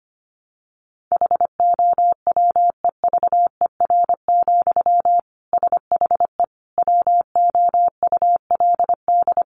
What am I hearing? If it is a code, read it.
Morse code, heard as a keyed tone, about 25 wpm, 700 Hz. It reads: HOWEVER, SHE WOULD
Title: Sentences from Top 700 Words 25wpm